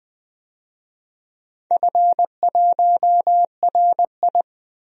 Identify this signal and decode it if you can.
Morse code, heard as a keyed tone, about 20 wpm, 700 Hz. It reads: F1RI
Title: DXCC-100 Call Signs - 4 Characters - 20wpm